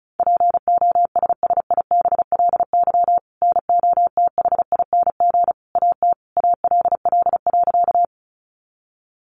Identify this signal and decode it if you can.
Morse code, heard as a keyed tone, about 34 wpm, 700 Hz. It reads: POSSIBLY NOTHING AT ALL.
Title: Sentences from Top 1300 Words 35wpm